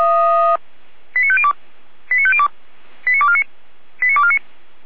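Five-tone selective call (ZVEI), heard as quick runs of five tones, tones of about 70 ms. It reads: 89502 89502 89249 89249
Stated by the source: ZVEI five-tone selective call received on an SDR.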